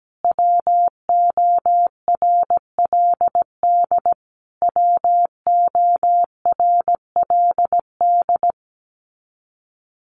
Morse code, heard as a keyed tone, about 17 wpm, 700 Hz. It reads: WORLD WORLD
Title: Top 200 Words Repeated 17wpm